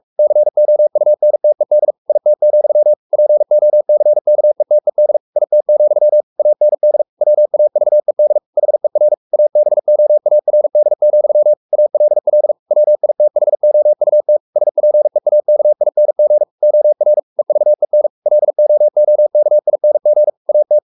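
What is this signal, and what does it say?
Morse code, heard as a keyed tone, about 44 wpm, 600 Hz. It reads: COUNTED IT, POCKETED IT, AND WAVED HER ABOARD, ALL WITHOUT SPEAKING OR EVEN LOOKING AT
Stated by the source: Morse code at 43 wpm sending Ken Follett's novel Never in English, from CW med Gnister.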